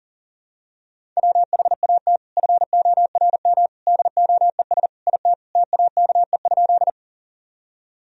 Morse code, heard as a keyed tone, about 40 wpm, 700 Hz. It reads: WHAT FORM DOES IT TAKE?
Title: Sentences from Top 1000 Words - Review Part 1 - 40wpm